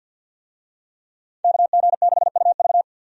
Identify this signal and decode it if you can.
Morse code, heard as a keyed tone, about 52 wpm, 700 Hz. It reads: KG6UV
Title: USA Call Signs - 5 Characters - 50wpm